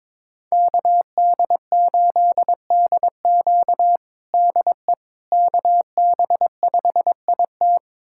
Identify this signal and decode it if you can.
Morse code, heard as a keyed tone, about 22 wpm, 700 Hz. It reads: KD8DQ DE KB5IT
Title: USA Callsign DE Callsign - 5 Characters - 22wpm